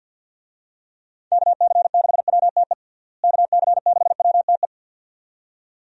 Morse code, heard as a keyed tone, about 45 wpm, 700 Hz. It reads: KC6WTE KC6WTE
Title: USA Call Signs - 6 Characters - Repeated 2x - 50wpm